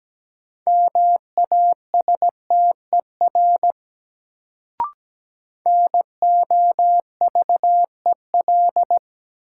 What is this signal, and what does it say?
Morse code, heard as a keyed tone, about 17 wpm, 700 Hz. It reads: MASTER NOVEL
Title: Sentences from Top 1300 Words - New Word Drill - 17wpm